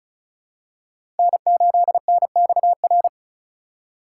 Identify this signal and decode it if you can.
Morse code, heard as a keyed tone, about 34 wpm, 700 Hz. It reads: N8NXR
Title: USA Call Signs - 5 Characters - 3x Character Spacing - 35wpm